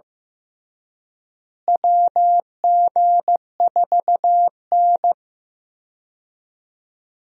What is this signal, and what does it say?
Morse code, heard as a keyed tone, about 15 wpm, 700 Hz. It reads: WG4N
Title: USA Call Signs - 4 Characters - Repeated 3x - 15wpm